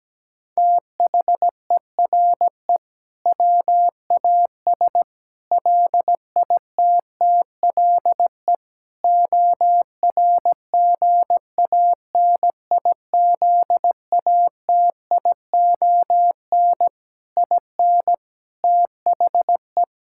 Morse code, heard as a keyed tone, about 17 wpm, 700 Hz. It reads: THERE WAS LITTLE ORGANIZATION IN THE
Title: Sentences from Top 1000 Words - Review Part 5 - 17wpm